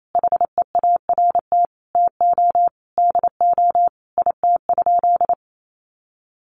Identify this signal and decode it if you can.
Morse code, heard as a keyed tone, about 28 wpm, 700 Hz. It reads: HEART TO DO IT?
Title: Sentences from Top 1200 Words 28wpm